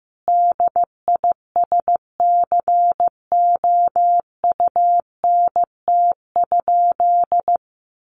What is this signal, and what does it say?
Morse code, heard as a keyed tone, about 15 wpm, 700 Hz. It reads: DISCOUNT?
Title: Sentences from Top 1200 Words 15wpm